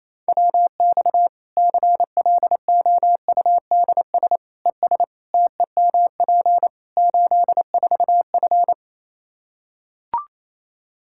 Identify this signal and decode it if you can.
Morse code, heard as a keyed tone, about 28 wpm, 700 Hz. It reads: WX CLOUDS ES TEMP 84F
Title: Practice QSO - Exchange 2 - Info - 28wpm